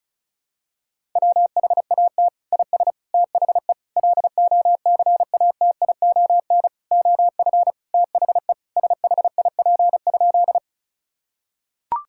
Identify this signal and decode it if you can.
Morse code, heard as a keyed tone, about 34 wpm, 700 Hz. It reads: WHAT IS THE LOCATION OF THE SHIP?
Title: Sentences from Top 1000 Words - Review Part 6 - 35wpm